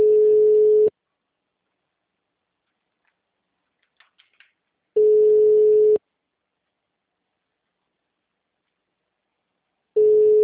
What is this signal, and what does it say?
Ringback tone (Europe, 425 Hz).